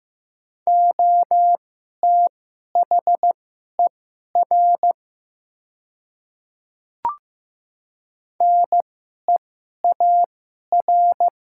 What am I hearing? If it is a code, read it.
Morse code, heard as a keyed tone, about 15 wpm, spaced as for 11 wpm, 700 Hz. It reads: OTHER NEAR
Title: 2x Character Spacing - Top 200 Words - 15wpm